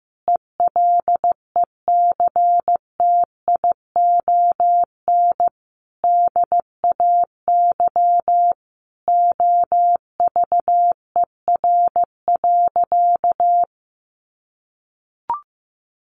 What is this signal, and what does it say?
Morse code, heard as a keyed tone, about 15 wpm, 700 Hz. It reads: ELECTION DAY OVER.
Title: A Sound Of Thunder 15wpm